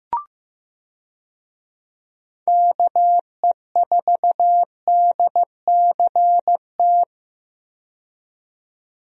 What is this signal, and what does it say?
Morse code, heard as a keyed tone, about 15 wpm, 700 Hz. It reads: KE4DCT
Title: USA Call Signs - 6 Characters - 15wpm